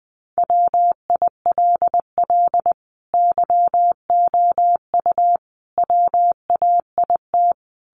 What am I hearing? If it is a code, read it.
Morse code, heard as a keyed tone, about 20 wpm, 700 Hz. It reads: WILL YOU WAIT
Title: Sentences from Top 900 Words 20wpm